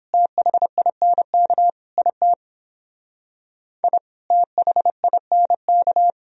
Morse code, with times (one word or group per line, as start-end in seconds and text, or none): 0.1-1.7 THINK
2.0-2.3 IT
3.8-4.0 I
4.3-6.1 THINK